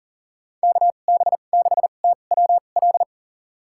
Morse code, heard as a keyed tone, about 40 wpm, spaced as for 29 wpm, 700 Hz. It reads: KB6TWL
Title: USA Call Signs - 6 Characters - 2x Character Spacing - 40wpm